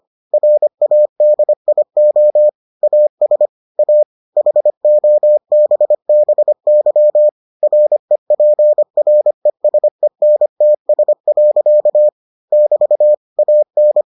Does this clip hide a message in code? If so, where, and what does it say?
Yes: Morse code from 0.3 to 14.0 s, heard as a keyed tone, about 25 wpm, 600 Hz. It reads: RADIO AS A HOBBY REPRESENTS. <BT> AN